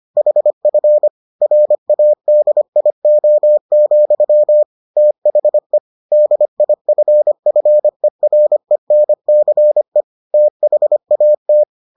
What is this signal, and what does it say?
Morse code, heard as a keyed tone, about 25 wpm, 600 Hz. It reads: HF RADIO, THE DIFFERENCE THAT